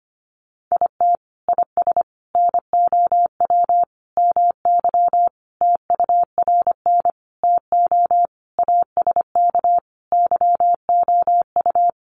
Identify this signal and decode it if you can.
Morse code, heard as a keyed tone, about 25 wpm, 700 Hz. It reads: IT IS NOW MY TURN TO ASK YOU
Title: Sentences from Top 200 Words 25wpm